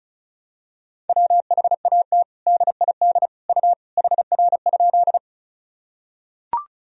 Morse code, heard as a keyed tone, about 36 wpm, 700 Hz. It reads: WHAT DID U HR?
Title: Sentences from Top 300 Words and Abbreviations 35wpm